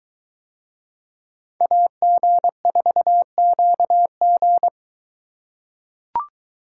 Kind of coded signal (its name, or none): Morse code